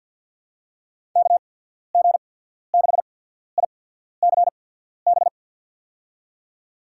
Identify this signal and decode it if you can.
Morse code, heard as a keyed tone, about 51 wpm, spaced as for 14 wpm, 700 Hz. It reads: KG6ICB
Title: USA Call Signs - 6 Characters - 8x Character Spacing - 50wpm